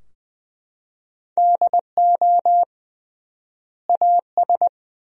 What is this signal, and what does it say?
Morse code, heard as a keyed tone, about 20 wpm, 700 Hz. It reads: DO AS
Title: 3x Word-Spacing - Sentences from Top 100 Words - 20wpm